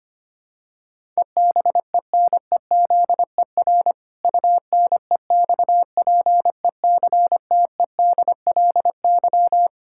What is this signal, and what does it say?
Morse code, heard as a keyed tone, about 25 wpm, 700 Hz. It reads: EBENEZER UNEXPECTEDLY
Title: Pangram Challenge v2 - 25wpm